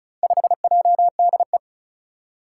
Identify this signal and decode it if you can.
Morse code, heard as a keyed tone, about 36 wpm, 700 Hz. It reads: 5JDE
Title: Four Challenging Letter-Number Combinations 35wpm